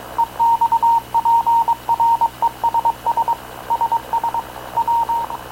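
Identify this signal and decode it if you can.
Morse code, heard as a keyed tone, about 22 wpm, 930 Hz. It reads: EXPRESS SS W
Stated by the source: Havana Radio / CLA sending Spanish press (PX) in Morse, recorded by Radio Officer Paul Dumesnil onboard CNAV QUEST/CZDO.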